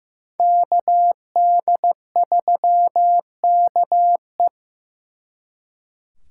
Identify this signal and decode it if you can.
Morse code, heard as a keyed tone, about 18 wpm, 700 Hz. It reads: KD3KE
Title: Speed Racing - USA 2x2 Call Signs 15wpm